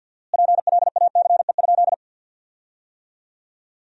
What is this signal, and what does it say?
Morse code, heard as a keyed tone, about 50 wpm, 700 Hz. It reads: PLACE?
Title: 4x Word-Spacing - Sentences from Top 200 Words - 50wpm